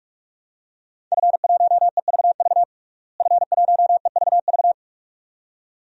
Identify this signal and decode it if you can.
Morse code, heard as a keyed tone, about 45 wpm, 700 Hz. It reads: F1EVV F1EVV
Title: DXCC-100 Call Signs - 5 Characters - Repeated 2x - 45wpm